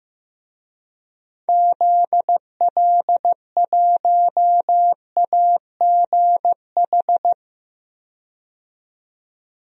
Morse code, heard as a keyed tone, about 15 wpm, 700 Hz. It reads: ZL1AGH
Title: DXCC-100 Call Signs - 6 Characters - 15wpm